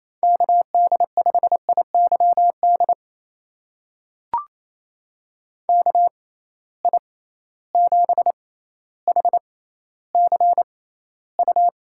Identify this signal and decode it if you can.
Morse code, heard as a keyed tone, about 28 wpm, spaced as for 14 wpm, 700 Hz. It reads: KD5IYD KI7HCU
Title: USA Call Signs - 6 Characters - 6x Character Spacing - 28wpm